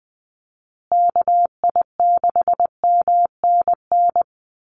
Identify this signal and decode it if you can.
Morse code, heard as a keyed tone, about 20 wpm, 700 Hz. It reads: KI6MNN
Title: USA Call Signs Repeated 2x 20wpm